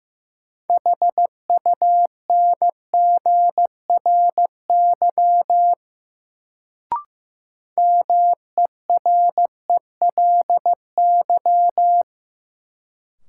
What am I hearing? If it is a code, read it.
Morse code, heard as a keyed tone, about 15 wpm, 700 Hz. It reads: HUNGRY MERELY
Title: Sentences from Top 1400 Words - New Word Drill - 15wpm